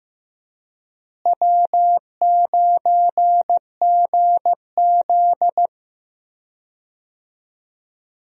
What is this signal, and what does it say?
Morse code, heard as a keyed tone, about 15 wpm, 700 Hz. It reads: W9GZ
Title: USA Call Signs - 4 Characters - Repeated 3x - 15wpm